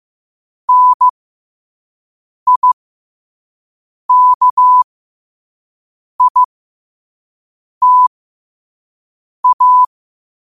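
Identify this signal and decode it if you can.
Morse code, heard as a keyed tone, about 15 wpm, spaced as for 5 wpm, 1000 Hz. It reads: NIKITA
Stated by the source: Morse at 5 wpm.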